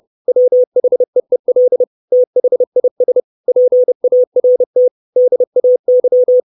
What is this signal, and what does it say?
Morse code, heard as a keyed tone, about 30 wpm, 500 Hz. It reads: WHEEL THIS PART DAY
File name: 632 Words 30WPM